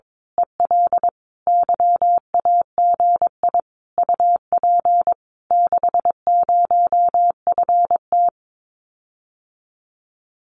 Morse code, heard as a keyed tone, about 22 wpm, 700 Hz. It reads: EL YAGI UP 60FT
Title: Common QSO Phrases Encore 22wpm